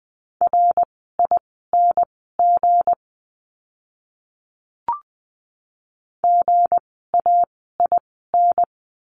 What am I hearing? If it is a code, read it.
Morse code, heard as a keyed tone, about 20 wpm, spaced as for 15 wpm, 700 Hz. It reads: RING GAIN
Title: 160 Lesson 26f - Letter g - Words 20wpm